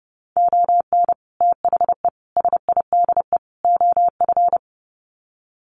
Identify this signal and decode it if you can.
Morse code, heard as a keyed tone, about 30 wpm, 700 Hz. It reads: ON THE SIDE OF
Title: Sets of 4 Words 30wpm